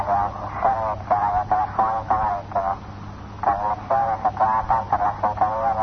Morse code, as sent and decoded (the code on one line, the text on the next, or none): none